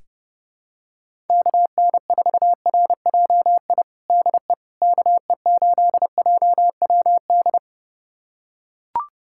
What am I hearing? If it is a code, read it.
Morse code, heard as a keyed tone, about 32 wpm, 700 Hz. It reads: KN4RJI DE KE8JWD
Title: USA Callsign DE Callsign - 6 Characters - 30wpm